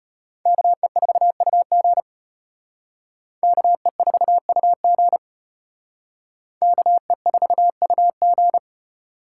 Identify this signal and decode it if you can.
Morse code, heard as a keyed tone, about 31 wpm, 700 Hz. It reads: KE4UG KE4UG KE4UG